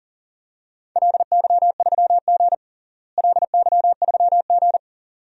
Morse code, heard as a keyed tone, about 40 wpm, 690 Hz. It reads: LY3G LY3G
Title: DXCC-100 Call Signs - 4 Characters - Repeated 2x - 40wpm